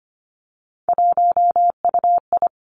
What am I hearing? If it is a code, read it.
Morse code, heard as a keyed tone, about 25 wpm, 700 Hz. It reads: 1UI
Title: Three Challenging Letter-Number Combinations 25wpm